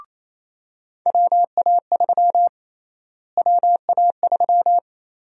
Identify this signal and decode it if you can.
Morse code, heard as a keyed tone, about 28 wpm, 700 Hz. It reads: WA3 WA3
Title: USA Call Sign Prefixes - Repeated 2x - 28wpm